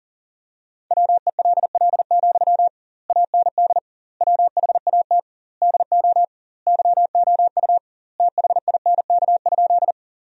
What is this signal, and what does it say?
Morse code, heard as a keyed tone, about 37 wpm, 700 Hz. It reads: WELL, AND WHAT DO YOU THINK?
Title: 2x Word-Spacing - Sentences from Top 200 Words - 40wpm